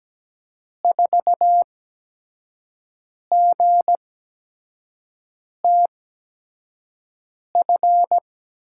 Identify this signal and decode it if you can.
Morse code, heard as a keyed tone, about 17 wpm, spaced as for 5 wpm, 700 Hz. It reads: 4GTF
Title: USA Call Signs - 5 Characters - 8x Character Spacing - 17wpm